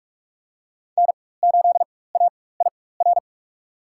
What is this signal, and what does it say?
Morse code, heard as a keyed tone, about 46 wpm, spaced as for 21 wpm, 700 Hz. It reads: N8AIR
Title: USA Call Signs - 5 Characters - 4x Character Spacing - 45wpm